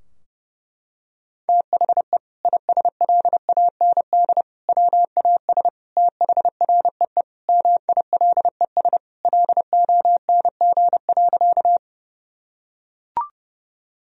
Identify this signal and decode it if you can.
Morse code, heard as a keyed tone, about 30 wpm, 700 Hz. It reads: THE ISLAND WAS THREE MILES LONG.
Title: Sentences from Top 1000 Words - Review Part 3 - 30wpm